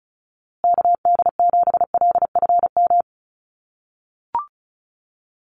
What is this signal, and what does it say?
Morse code, heard as a keyed tone, about 34 wpm, 700 Hz. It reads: KD7LFM
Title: USA Call Signs Repeated 2x 35wpm